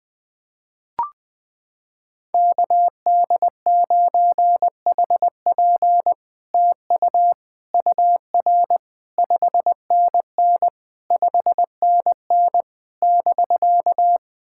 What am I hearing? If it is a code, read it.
Morse code, heard as a keyed tone, about 20 wpm, 700 Hz. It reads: KD9HP TU UR 5NN 5NN <BK>
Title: SOTA QSO Practice 20wpm